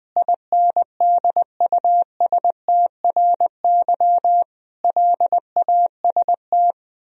Morse code, heard as a keyed tone, about 21 wpm, 700 Hz. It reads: INDUSTRY LAST
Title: Sentences from Top 1000 Words - Review Part 4 - 20wpm